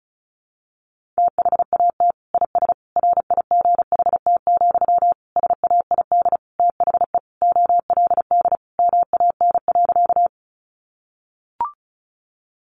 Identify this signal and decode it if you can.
Morse code, heard as a keyed tone, about 34 wpm, 700 Hz. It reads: THAT IS RIGHT, SAID THE OLD MAN.
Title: Sentences from Top 200 Words 35wpm